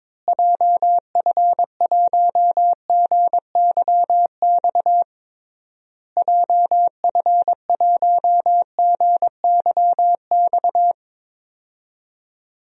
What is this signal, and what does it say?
Morse code, heard as a keyed tone, about 22 wpm, 700 Hz. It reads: JF1GYX JF1GYX
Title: DXCC-100 Call Signs - 6 Characters - Repeated 3x - 22wpm